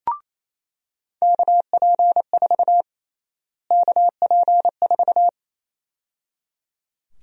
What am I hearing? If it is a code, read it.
Morse code, heard as a keyed tone, about 28 wpm, 700 Hz. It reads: KP4 KP4
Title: USA Call Sign Prefixes - Repeated 2x - 28wpm